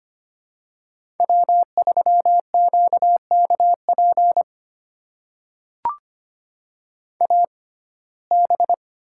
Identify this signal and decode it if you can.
Morse code, heard as a keyed tone, about 25 wpm, spaced as for 18 wpm, 700 Hz. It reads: W3QKP AB
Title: USA Call Signs - 5 Characters - 6x Character Spacing - 25wpm